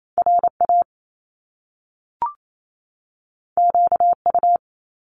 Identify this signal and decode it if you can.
Morse code, heard as a keyed tone, about 28 wpm, 700 Hz. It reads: RA QU
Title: Top Two Letter Combinations in Words 28wpm